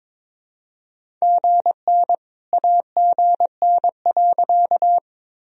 Morse code, heard as a keyed tone, about 22 wpm, 700 Hz. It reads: GN AGN.
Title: Sentences from Top 300 Words and Abbreviations 22wpm